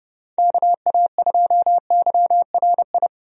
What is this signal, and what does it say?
Morse code, heard as a keyed tone, about 32 wpm, 700 Hz. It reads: KA2YRI
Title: USA Call Signs - 6 Characters - 30wpm